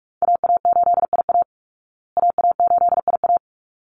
Morse code, heard as a keyed tone, about 44 wpm, 700 Hz. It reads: AA8IA AA8IA
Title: Call Signs Repeated 3x - 45wpm